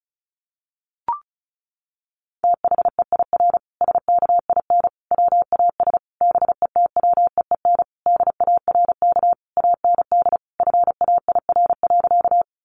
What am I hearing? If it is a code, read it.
Morse code, heard as a keyed tone, about 34 wpm, 700 Hz. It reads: THEIR SKIN WAS BETWEEN DARK AND FAIR.
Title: Sentences from Top 1300 Words 35wpm